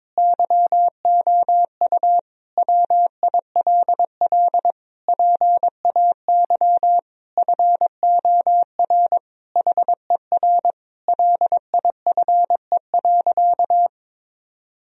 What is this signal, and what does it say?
Morse code, heard as a keyed tone, about 22 wpm, 700 Hz. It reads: YOU WILL PAY FOR HER LIFE.
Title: Sentences from Top 1000 Words - Review Part 4 - 22wpm